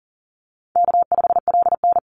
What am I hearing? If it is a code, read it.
Morse code, heard as a keyed tone, about 40 wpm, 700 Hz. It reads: K5LN
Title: Easy Call Signs - 40wpm